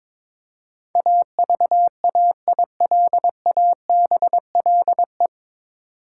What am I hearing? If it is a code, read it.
Morse code, heard as a keyed tone, about 22 wpm, 690 Hz. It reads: AVAILABLE
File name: 9 Letter Words - First 3 Letters Given - 22wpm